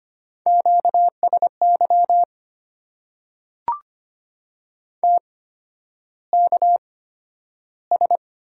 Morse code, heard as a keyed tone, about 25 wpm, spaced as for 11 wpm, 700 Hz. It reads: QSY TKS
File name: Top 90 QSO Elements - 8x Character Spacing - 25wpm